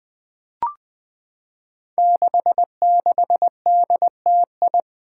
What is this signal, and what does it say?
Morse code, heard as a keyed tone, about 20 wpm, 700 Hz. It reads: 66DTI
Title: Five Challenging Letter-Number Combinations 20wpm